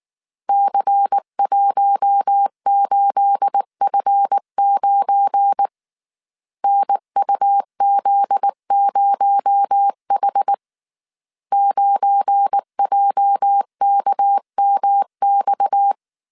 Morse code, heard as a keyed tone, about 19 wpm, 800 Hz. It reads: C18F9 NUZ0H 9JKMX
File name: CW / DET / 1 9 1500C S A 18wpm 2459